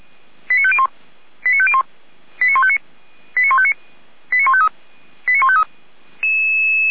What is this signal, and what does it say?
Five-tone selective call (ZVEI), heard as quick runs of five tones, tones of about 70 ms. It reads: 89501 89501 89159 89159 89153 89153
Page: /455/SDR/ZVEI/89168/